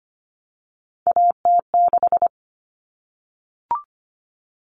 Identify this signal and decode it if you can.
Morse code, heard as a keyed tone, about 25 wpm, 700 Hz. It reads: AT6
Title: Common Cut Numbers - 3-Digit 25wpm